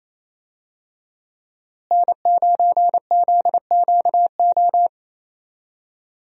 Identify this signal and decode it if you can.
Morse code, heard as a keyed tone, about 28 wpm, 700 Hz. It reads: N9ZQO
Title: USA Call Signs - 5 Characters - Repeated 2x - 28wpm